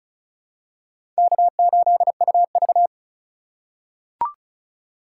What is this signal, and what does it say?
Morse code, heard as a keyed tone, about 36 wpm, 700 Hz. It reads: K8UV
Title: USA Call Signs - 4 Characters - 4x Character Spacing - 35wpm